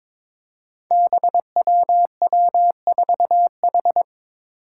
Morse code, heard as a keyed tone, about 22 wpm, 700 Hz. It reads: BWW4H
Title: Five Challenging Letter-Number Combinations 22wpm